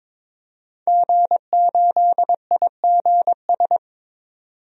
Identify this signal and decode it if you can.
Morse code, heard as a keyed tone, about 22 wpm, 700 Hz. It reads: G8IGS
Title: DXCC-100 Call Signs - 5 Characters - 22wpm